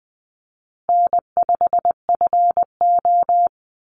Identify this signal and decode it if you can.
Morse code, heard as a keyed tone, about 20 wpm, 700 Hz. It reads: N5FO
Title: Easy Call Signs Repeated 3x - 20wpm